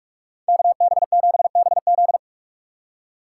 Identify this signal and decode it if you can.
Morse code, heard as a keyed tone, about 44 wpm, 700 Hz. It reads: KB7BZ